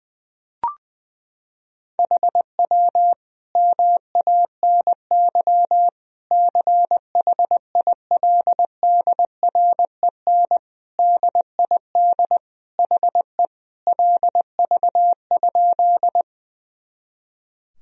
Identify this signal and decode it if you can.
Morse code, heard as a keyed tone, about 21 wpm, 700 Hz. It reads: HW MANY CHILDREN DID HE LV?
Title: Sentences from Top 400 Words and Abbreviations 20wpm